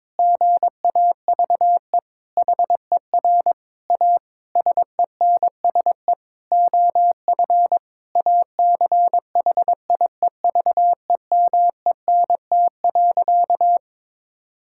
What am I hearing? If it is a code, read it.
Morse code, heard as a keyed tone, about 22 wpm, 700 Hz. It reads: GAVE HER A SENSE OF ACHIEVEMENT.
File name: Difficult Sentences with Hints - 22wpm